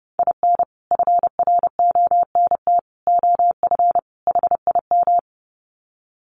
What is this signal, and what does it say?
Morse code, heard as a keyed tone, about 29 wpm, 700 Hz. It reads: IN FRONT OF HIM
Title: Sets of 4 Words 30wpm